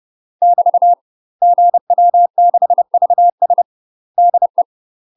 Morse code, heard as a keyed tone, about 30 wpm, 700 Hz. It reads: <BT> GW6VS DE
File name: UK MASTER QSOs TRAINING / 30WPM / UK MASTER QSOs 0001